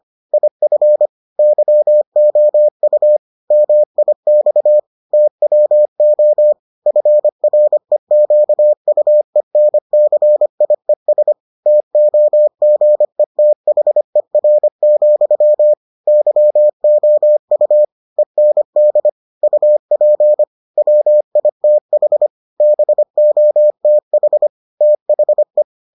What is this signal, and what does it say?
Morse code, heard as a keyed tone, about 25 wpm, 600 Hz. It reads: IF YOU MIX TWO FREQUENCIES TOGETHER, YOU END UP WITH BOTH THE